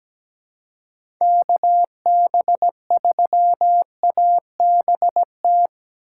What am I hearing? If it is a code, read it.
Morse code, heard as a keyed tone, about 17 wpm, 700 Hz. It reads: KB3ABT
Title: USA Call Signs - 6 Characters - 4x Character Spacing - 17wpm